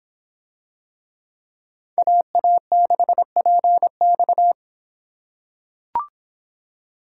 Morse code, heard as a keyed tone, about 25 wpm, 700 Hz. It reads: AA6PX